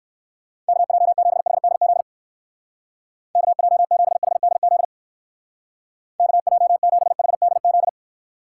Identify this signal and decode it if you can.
Morse code, heard as a keyed tone, about 58 wpm, 700 Hz. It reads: KJ7HDZ KJ7HDZ KJ7HDZ